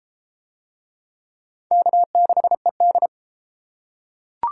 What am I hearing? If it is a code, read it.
Morse code, heard as a keyed tone, about 28 wpm, 700 Hz. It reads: K6ED